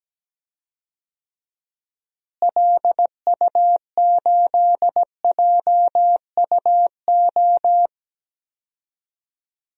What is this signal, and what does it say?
Morse code, heard as a keyed tone, about 17 wpm, 700 Hz. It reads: LU8JUO
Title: DXCC-100 Call Signs - 6 Characters - Repeated 3x - 17wpm